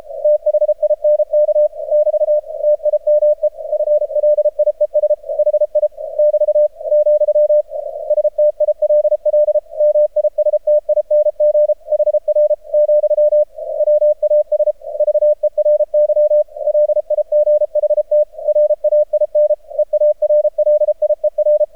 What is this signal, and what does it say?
Morse code, heard as a keyed tone, about 33 wpm, 600 Hz, with radio noise. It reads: THINK <BT> TIME FLIES HI <BT> , STILL MISTING HR , WAS VERY LIGHT RAIN EARLIER
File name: archived / CRSnet